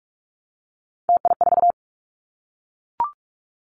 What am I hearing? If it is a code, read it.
Morse code, heard as a keyed tone, about 44 wpm, 700 Hz. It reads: TI4
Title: Callsign Prefixes for Top 100 DXCC - 45wpm